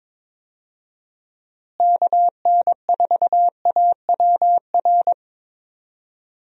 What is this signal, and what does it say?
Morse code, heard as a keyed tone, about 21 wpm, 700 Hz. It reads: KN4AWR
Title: USA Call Signs - 6 Characters - Repeated 2x - 22wpm